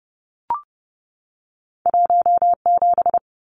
Morse code, heard as a keyed tone, about 30 wpm, 700 Hz. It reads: 17
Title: Numbers - 2-Digit 30wpm